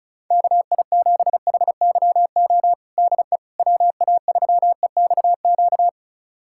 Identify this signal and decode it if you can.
Morse code, heard as a keyed tone, about 34 wpm, 700 Hz. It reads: KI7HYO DE WA3EXQ
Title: USA Callsign DE Callsign - 6 Characters - 35wpm